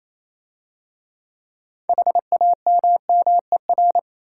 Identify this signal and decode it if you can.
Morse code, heard as a keyed tone, about 27 wpm, 690 Hz. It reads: HAMMER
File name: Easy Words - Speed-Racing with Hints - 22wpm